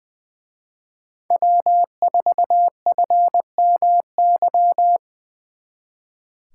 Morse code, heard as a keyed tone, about 24 wpm, 700 Hz. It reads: W4FMY
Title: Speed Racing - USA 1x3 Call Signs 20wpm